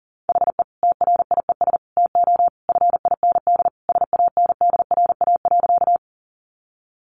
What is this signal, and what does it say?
Morse code, heard as a keyed tone, about 40 wpm, 700 Hz. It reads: HE TRIES TO FIND SANDRA.
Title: Easy Phrases 40wpm